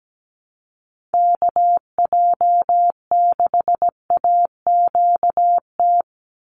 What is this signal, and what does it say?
Morse code, heard as a keyed tone, about 17 wpm, 700 Hz. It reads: KJ6AQT
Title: USA Call Signs 17wpm